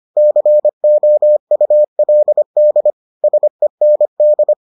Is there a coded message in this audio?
Yes: Morse code from 0.2 to 4.5 s, heard as a keyed tone, about 25 wpm, 600 Hz. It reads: COULD SEND